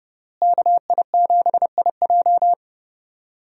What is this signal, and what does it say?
Morse code, heard as a keyed tone, about 30 wpm, 700 Hz. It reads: KI7IJ
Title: USA Call Signs - 5 Characters - Repeated 3x - 30wpm